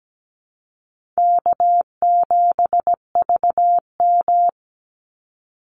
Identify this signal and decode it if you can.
Morse code, heard as a keyed tone, about 17 wpm, 700 Hz. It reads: K7VM
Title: Easy Call Signs - 17wpm